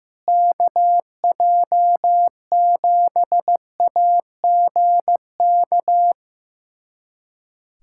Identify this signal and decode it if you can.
Morse code, heard as a keyed tone, about 15 wpm, 700 Hz. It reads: KJ7AGK
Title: USA Callsign DE Callsign 15wpm